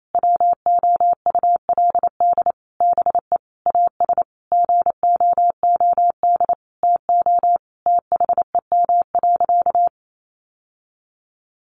Morse code, heard as a keyed tone, about 28 wpm, 700 Hz. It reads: WOULD BE AS GOOD TO THEM.
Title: Sentences from Top 1200 Words 28wpm